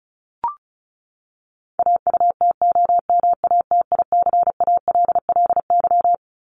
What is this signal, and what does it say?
Morse code, heard as a keyed tone, about 34 wpm, 700 Hz. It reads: AUTOMATICALLY
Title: Top 1751-2000 Words 35wpm